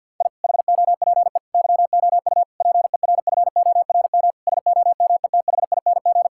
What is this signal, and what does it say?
Morse code, heard as a keyed tone, about 50 wpm, 700 Hz. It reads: I HOPE YOU PERFORM SOMETHING
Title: Sentences from Top 1000 Words - Review Part 6 - 50wpm